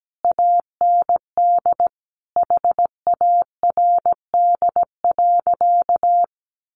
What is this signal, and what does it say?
Morse code, heard as a keyed tone, about 17 wpm, 700 Hz. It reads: AND HARD.
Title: Sentences from Top 1300 Words 17wpm